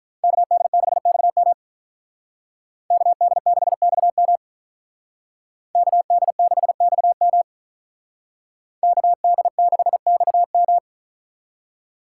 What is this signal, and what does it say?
Morse code, heard as a keyed tone, about 45 wpm, 700 Hz. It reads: KD6XM KD6XM KD6XM KD6XM